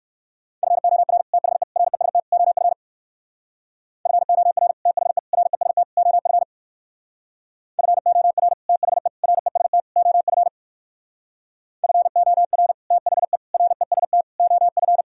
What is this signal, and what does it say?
Morse code, heard as a keyed tone, about 67 wpm, 700 Hz. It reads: FOR THE REST OF FOR THE REST OF FOR THE REST OF FOR THE REST OF